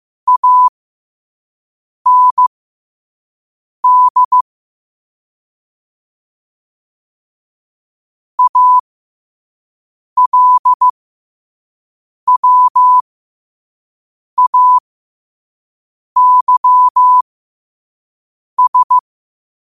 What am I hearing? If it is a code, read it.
Morse code, heard as a keyed tone, about 15 wpm, spaced as for 5 wpm, 1000 Hz. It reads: AND ALWAYS
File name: qotdfiles / 5wpm